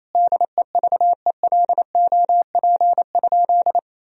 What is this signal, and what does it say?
Morse code, heard as a keyed tone, about 28 wpm, 700 Hz. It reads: DEVELOP?
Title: Sentences from Top 1000 Words - Review Part 2 - 28wpm